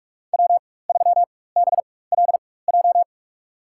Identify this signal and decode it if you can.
Morse code, heard as a keyed tone, about 46 wpm, spaced as for 21 wpm, 700 Hz. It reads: W3BLJ
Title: USA Call Signs - 5 Characters - 4x Character Spacing - 45wpm